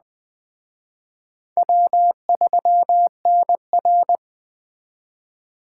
Morse code, heard as a keyed tone, about 20 wpm, 700 Hz. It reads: W3NR
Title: Speed Racing - USA 1x2 Call Signs 17wpm